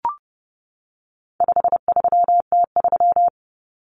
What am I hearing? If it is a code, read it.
Morse code, heard as a keyed tone, about 30 wpm, 700 Hz. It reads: 53T3
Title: Common Cut Numbers - 4-Digit 30wpm